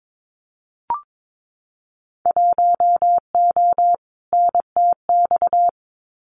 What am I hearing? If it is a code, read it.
Morse code, heard as a keyed tone, about 22 wpm, 700 Hz. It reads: 1O NTX
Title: Winter Field Day Exchanges Encore 22wpm